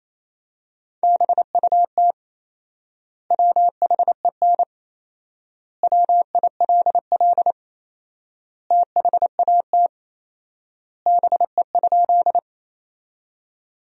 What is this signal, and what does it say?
Morse code, heard as a keyed tone, about 28 wpm, 700 Hz. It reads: BUT WHEN WILL THAT BE?
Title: 4x Word-Spacing - Sentences from Top 100 Words - 28wpm